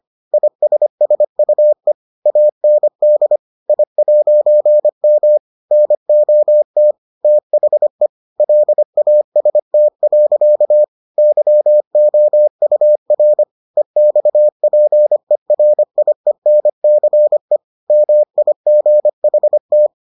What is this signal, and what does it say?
Morse code, heard as a keyed tone, about 25 wpm, 600 Hz. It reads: ISSUE AND I'M NOT THE LAST. YOUR EXPERIENCE MIGHT